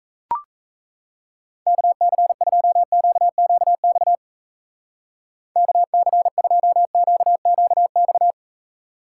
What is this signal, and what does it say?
Morse code, heard as a keyed tone, about 33 wpm, 700 Hz. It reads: KC2QQX KC2QQX
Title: Speed Racing - USA 2x3 Call Signs 28wpm